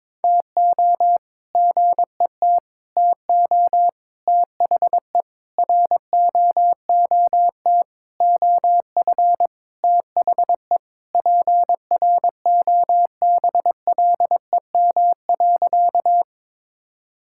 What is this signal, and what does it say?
Morse code, heard as a keyed tone, about 22 wpm, 700 Hz. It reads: TO GET TO THE ROOT OF THE PROBLEM.
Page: morse-code-ninja-practice-22wpm